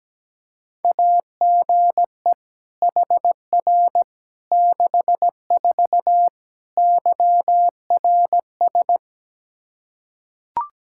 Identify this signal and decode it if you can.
Morse code, heard as a keyed tone, about 17 wpm, 700 Hz. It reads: AGE HR 64 YRS
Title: K1USN SST QSO Practice 17wpm